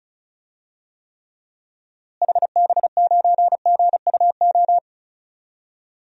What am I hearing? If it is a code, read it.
Morse code, heard as a keyed tone, about 34 wpm, 700 Hz. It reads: HB9GUO
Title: DXCC-100 Call Signs - 6 Characters - Repeated 3x - 35wpm